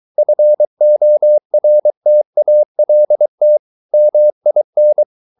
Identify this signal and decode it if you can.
Morse code, heard as a keyed tone, about 23 wpm, 600 Hz. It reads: FORTALT MIN
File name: Ferie QRS 23wpm